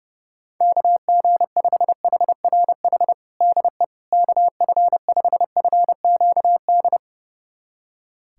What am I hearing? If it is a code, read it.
Morse code, heard as a keyed tone, about 32 wpm, 700 Hz. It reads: KG5HRH DE KF5FQD
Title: USA Callsign DE Callsign - 6 Characters - 30wpm